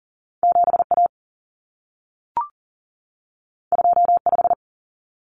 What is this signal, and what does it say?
Morse code, heard as a keyed tone, about 40 wpm, 700 Hz. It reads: 7A 25
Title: Common Cut Numbers - 2-Digit 40wpm